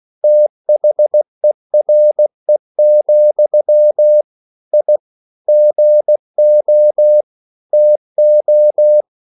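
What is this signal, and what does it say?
Morse code, heard as a keyed tone, about 16 wpm, 600 Hz. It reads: THERE, I GO TO
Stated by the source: English quotations sent in CW as practice at 16 words per minute.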